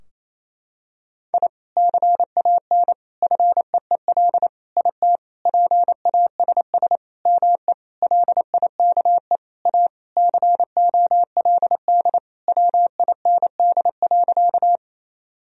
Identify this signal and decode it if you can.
Morse code, heard as a keyed tone, about 28 wpm, 700 Hz. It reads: I CAN FEEL IT PASS ME LIKE A COLD WIND.
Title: Sentences from Top 1000 Words - Review Part 2 - 28wpm